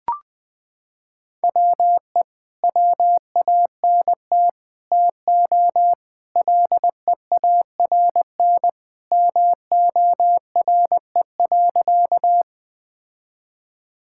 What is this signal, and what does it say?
Morse code, heard as a keyed tone, about 20 wpm, 700 Hz. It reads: WE WANT TO LEARN MORE.
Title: Sentences with 3-Word Phrases - 20wpm